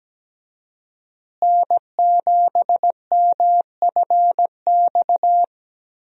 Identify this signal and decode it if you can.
Morse code, heard as a keyed tone, about 17 wpm, 700 Hz. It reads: N7MFX
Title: USA Call Signs - 5 Characters - Repeated 2x - 17wpm